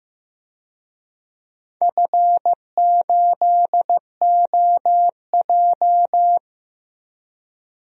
Morse code, heard as a keyed tone, about 15 wpm, 700 Hz. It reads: F8OJ
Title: DXCC-100 Call Signs - 4 Characters - Repeated 3x - 15wpm